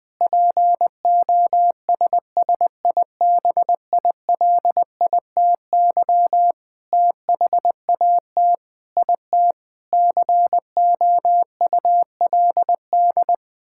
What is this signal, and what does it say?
Morse code, heard as a keyed tone, about 20 wpm, 700 Hz. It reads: POSSIBILITY THAT IT COULD
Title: Sentences with 9-Word Phrases - 20wpm